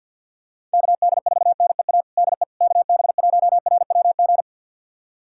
Morse code, heard as a keyed tone, about 49 wpm, 700 Hz. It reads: KD4NEA DE KB1RWG
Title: USA Callsign DE Callsign - 6 Characters - 50wpm